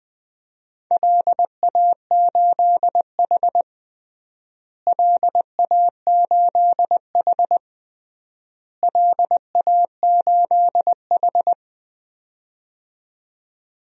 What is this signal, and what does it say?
Morse code, heard as a keyed tone, about 20 wpm, 700 Hz. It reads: LA8H LA8H LA8H
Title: DXCC-100 Call Signs - 4 Characters - Repeated 3x - 20wpm